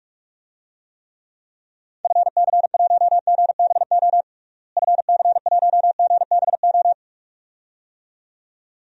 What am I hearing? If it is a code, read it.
Morse code, heard as a keyed tone, about 46 wpm, 700 Hz. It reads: FC1GBO FC1GBO
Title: DXCC-100 Call Signs - 6 Characters - Repeated 2x - 45wpm